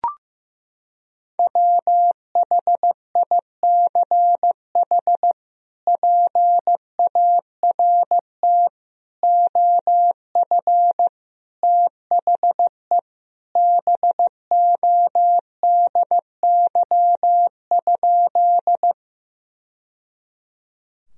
Morse code, heard as a keyed tone, about 15 wpm, 700 Hz. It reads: WHICH PART OF THE BODY?
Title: Sentences from Top 1000 Words - Review Part 2 - 15wpm